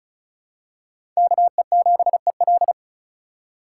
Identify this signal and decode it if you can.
Morse code, heard as a keyed tone, about 36 wpm, 690 Hz. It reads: KE7EL
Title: USA Call Signs - 5 Characters - 3x Character Spacing - 35wpm